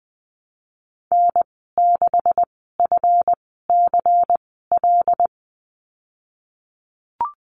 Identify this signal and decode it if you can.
Morse code, heard as a keyed tone, about 20 wpm, spaced as for 15 wpm, 700 Hz. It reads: N6FCL
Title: 154 Lesson 25g - Letter p - Call Signs 20wpm